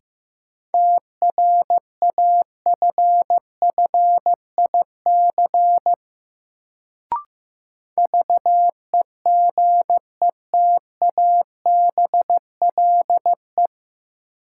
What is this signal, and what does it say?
Morse code, heard as a keyed tone, about 15 wpm, 700 Hz. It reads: TRAFFIC VEGETABLE
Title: Sentences from Top 1300 Words - New Word Drill - 15wpm